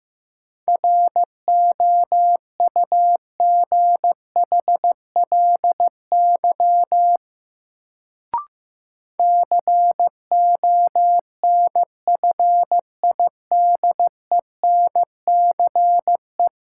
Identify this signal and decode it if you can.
Morse code, heard as a keyed tone, about 15 wpm, 700 Hz. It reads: ROUGHLY CONFIDENCE
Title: Sentences from Top 1500 Words - New Word Drill Alt - 15wpm